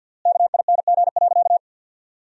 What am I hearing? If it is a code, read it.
Morse code, heard as a keyed tone, about 47 wpm, 700 Hz. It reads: KING.
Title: 3x Word-Spacing - Sentences from Top 400 Words - 50wpm